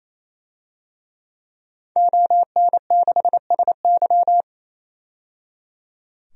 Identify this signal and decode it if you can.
Morse code, heard as a keyed tone, about 27 wpm, 700 Hz. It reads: ON6SY
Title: DXCC-100 Call Signs - 5 Characters - 28wpm